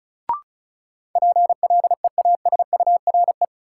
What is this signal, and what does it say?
Morse code, heard as a keyed tone, about 36 wpm, 700 Hz. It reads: PLEASURE
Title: Sentences from Top 1600 Words - New Word Drill Alt - 35wpm